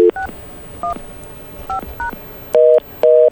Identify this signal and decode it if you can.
DTMF, heard as telephone touch tones, tones of about 120 ms. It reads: 615#